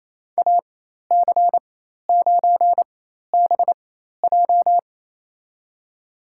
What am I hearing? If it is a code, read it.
Morse code, heard as a keyed tone, about 28 wpm, spaced as for 13 wpm, 700 Hz. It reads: AC9BJ
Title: USA Call Signs - 5 Characters - 4x Character Spacing - 28wpm